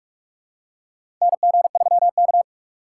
Morse code, heard as a keyed tone, about 43 wpm, 700 Hz. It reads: NG3K
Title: USA Call Signs - 4x Character Spacing - 45wpm